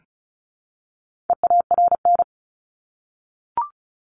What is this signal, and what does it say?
Morse code, heard as a keyed tone, about 34 wpm, 700 Hz. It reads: EARN
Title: Top 1251-1500 Words 35wpm